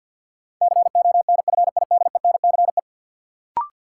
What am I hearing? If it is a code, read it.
Morse code, heard as a keyed tone, about 53 wpm, 700 Hz. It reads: CONFIDENCE
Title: Sentences from Top 1500 Words - New Word Drill - 50wpm